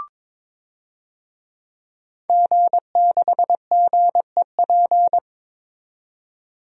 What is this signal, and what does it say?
Morse code, heard as a keyed tone, about 22 wpm, 700 Hz. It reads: G6GEP
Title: DXCC-100 Call Signs - 5 Characters - 22wpm